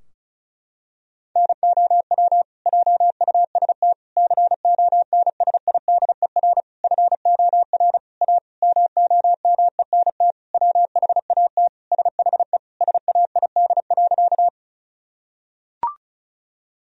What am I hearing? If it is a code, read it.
Morse code, heard as a keyed tone, about 36 wpm, 700 Hz. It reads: NOW JUST CONSIDER FOR A MOMENT WHAT SHE SAID.
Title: Sentences from Top 1000 Words - Review Part 4 - 35wpm